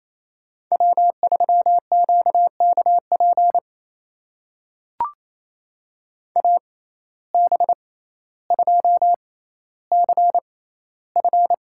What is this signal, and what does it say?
Morse code, heard as a keyed tone, about 28 wpm, spaced as for 14 wpm, 700 Hz. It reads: W3QKP AB2CF
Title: USA Call Signs - 5 Characters - 6x Character Spacing - 28wpm